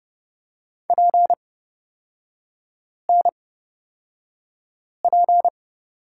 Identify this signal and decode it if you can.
Morse code, heard as a keyed tone, about 30 wpm, spaced as for 5 wpm, 700 Hz. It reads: PNP
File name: Lesson 37e - Letter x - Three Characters 30wpm